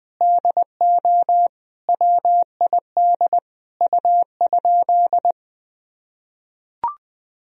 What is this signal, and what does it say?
Morse code, heard as a keyed tone, about 20 wpm, 700 Hz. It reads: DO WID U?
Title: Sentences from Top 100 Words and Abbreviations 20wpm